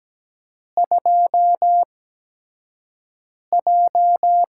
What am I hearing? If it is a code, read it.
Morse code, heard as a keyed tone, about 17 wpm, spaced as for 5 wpm, 700 Hz. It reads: 2J
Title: USA Call Signs - 4 Characters - 8x Character Spacing - 17wpm